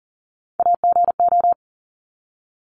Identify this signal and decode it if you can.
Morse code, heard as a keyed tone, about 40 wpm, 700 Hz. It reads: AGO
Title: Top 500 Words 40wpm